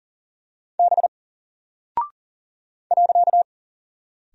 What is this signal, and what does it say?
Morse code, heard as a keyed tone, about 40 wpm, 700 Hz. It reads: B .
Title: Single Character (Letter:Number:Punctuation:Prosign) 40wpm